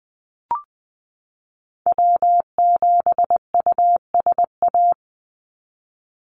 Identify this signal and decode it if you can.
Morse code, heard as a keyed tone, about 20 wpm, 700 Hz. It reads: W7USA
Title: NA SOTA Activator Callsigns 20wpm